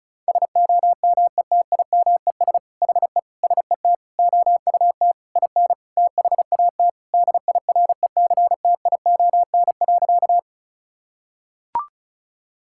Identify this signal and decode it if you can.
Morse code, heard as a keyed tone, about 34 wpm, 700 Hz. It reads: SOMETIMES HE SET OUT IN THAT DIRECTION.
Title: Sentences from Top 1000 Words - Review Part 5 - 35wpm